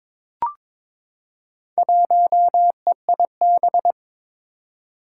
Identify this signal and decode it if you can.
Morse code, heard as a keyed tone, about 22 wpm, 700 Hz. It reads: 1EIB
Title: Four Challenging Letter-Number Combinations 22wpm